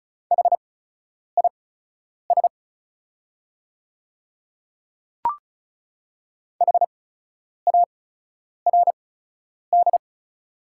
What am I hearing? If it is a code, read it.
Morse code, heard as a keyed tone, about 34 wpm, spaced as for 9 wpm, 690 Hz. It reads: HIS HARD
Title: 8x Character Spacing - Top 500 Words - 35wpm